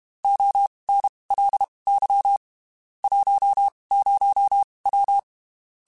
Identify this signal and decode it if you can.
Morse code, heard as a keyed tone, about 32 wpm, spaced as for 23 wpm, 800 Hz. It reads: ONLY 10W
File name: (spaced) / QSOs Spaced-mp3s / kc7haa @30 sp